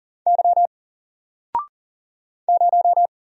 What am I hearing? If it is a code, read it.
Morse code, heard as a keyed tone, about 41 wpm, 700 Hz. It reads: Y 0
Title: Single Letter-Number Alt 40wpm